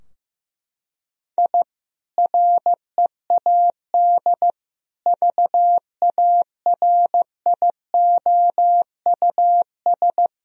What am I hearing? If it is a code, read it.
Morse code, heard as a keyed tone, about 15 wpm, 700 Hz. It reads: I READ VARIOUS
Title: Sentences from Top 1000 Words - Review Part 4 - 15wpm